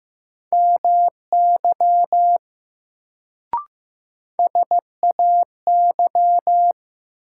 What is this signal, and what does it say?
Morse code, heard as a keyed tone, about 15 wpm, 700 Hz. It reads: MY SAY
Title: Sentences from Top 200 Words - New Word Drill - 15wpm